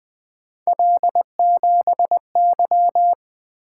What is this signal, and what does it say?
Morse code, heard as a keyed tone, about 20 wpm, 700 Hz. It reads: L7Y
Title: Three Letter-Number Combinations - 20wpm